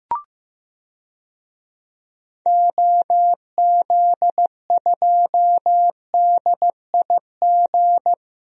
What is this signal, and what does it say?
Morse code, heard as a keyed tone, about 15 wpm, 700 Hz. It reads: OZ2DIG
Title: DXCC-100 Call Signs - 6 Characters - Repeated 2x - 15wpm